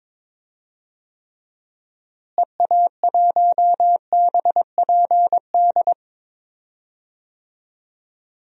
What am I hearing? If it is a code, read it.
Morse code, heard as a keyed tone, about 22 wpm, 700 Hz. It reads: EA1BPD
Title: DXCC-100 Call Signs - 6 Characters - Repeated 2x - 22wpm